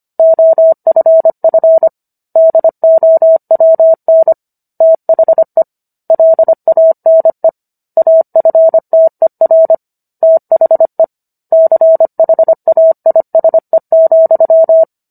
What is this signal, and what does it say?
Morse code, heard as a keyed tone, about 25 wpm, 650 Hz. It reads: OFF DOWN THE LANE AFTER THE CHAISE,